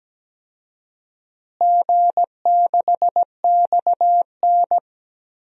Morse code, heard as a keyed tone, about 17 wpm, 700 Hz. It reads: G6XN
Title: DXCC-100 Call Signs - 4 Characters - Repeated 2x - 17wpm